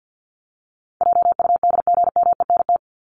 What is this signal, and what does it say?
Morse code, heard as a keyed tone, about 50 wpm, 700 Hz. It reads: JUDGMENT
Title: Top 1751-2000 Words 50wpm